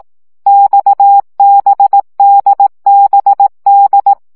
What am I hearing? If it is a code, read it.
Morse code, heard as a keyed tone, about 18 wpm, 800 Hz. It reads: XBDBD
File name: CW gyakorlas d b x 15 baud